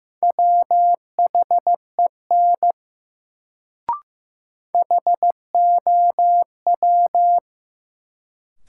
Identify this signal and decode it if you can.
Morse code, heard as a keyed tone, about 15 wpm, 700 Hz. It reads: WHEN HOW
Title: Sentences from Top 100 Words - New Word Drill - 15wpm